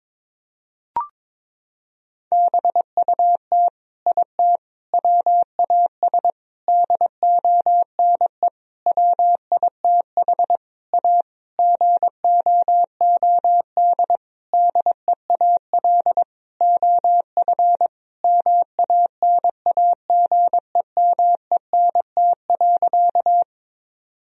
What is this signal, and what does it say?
Morse code, heard as a keyed tone, about 22 wpm, 700 Hz. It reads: BUT IT WAS DONE WITH A GOOD DEAL OF MANAGEMENT.
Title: Sentences from Top 1000 Words - Review Part 4 - 22wpm